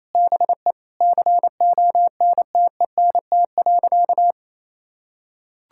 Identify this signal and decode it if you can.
Morse code, heard as a keyed tone, about 28 wpm, 700 Hz. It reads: BE CONTENT.
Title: Sentences from Top 1000 Words - Review Part 6 - 28wpm